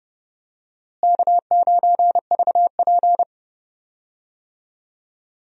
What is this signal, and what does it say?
Morse code, heard as a keyed tone, about 30 wpm, 700 Hz. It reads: K9VP
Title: USA Call Signs - 4 Characters - Repeated 2x - 30wpm